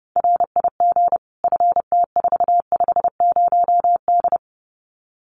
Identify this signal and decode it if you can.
Morse code, heard as a keyed tone, about 30 wpm, 700 Hz. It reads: RIG FT450D
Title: Common QSO Phrases 30wpm